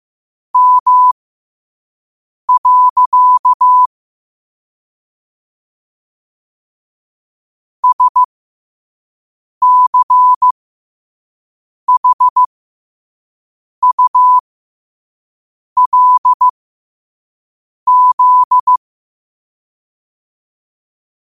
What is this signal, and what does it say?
Morse code, heard as a keyed tone, about 15 wpm, spaced as for 5 wpm, 1000 Hz. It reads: M. SCHULZ
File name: qotdfiles / 5wpm